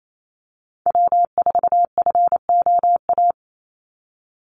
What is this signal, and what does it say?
Morse code, heard as a keyed tone, about 28 wpm, 700 Hz. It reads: W4FOA
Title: Call Signs Repeated 3x - 28wpm